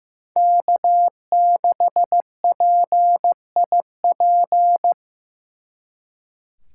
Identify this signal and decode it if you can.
Morse code, heard as a keyed tone, about 18 wpm, 700 Hz. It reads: K6PIP
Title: Speed Racing - USA 1x3 Call Signs 15wpm